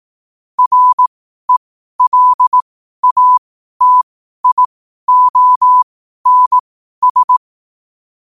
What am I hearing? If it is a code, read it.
Morse code, heard as a keyed tone, about 18 wpm, spaced as for 12 wpm, 1000 Hz. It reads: RELATIONS
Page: 12-wpm-morse-code-quotes